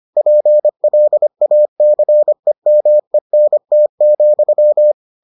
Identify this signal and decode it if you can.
Morse code, heard as a keyed tone, about 25 wpm, 600 Hz. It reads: PLACEMENT,